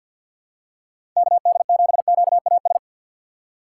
Morse code, heard as a keyed tone, about 47 wpm, 700 Hz. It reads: KD7QAS
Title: USA Call Signs - 6 Characters - 8x Character Spacing - 50wpm